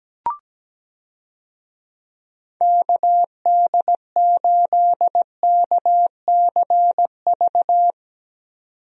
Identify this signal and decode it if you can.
Morse code, heard as a keyed tone, about 17 wpm, 700 Hz. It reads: KD8KCV
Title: USA Call Signs - 6 Characters - 17wpm